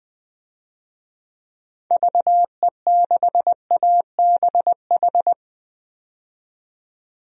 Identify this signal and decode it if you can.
Morse code, heard as a keyed tone, about 20 wpm, 700 Hz. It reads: VE6ABH
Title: DXCC-100 Call Signs - 6 Characters - 20wpm